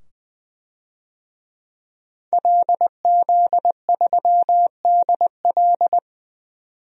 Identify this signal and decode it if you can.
Morse code, heard as a keyed tone, about 21 wpm, 700 Hz. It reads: LZ3DL
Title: DXCC-100 Call Signs - 5 Characters - 20wpm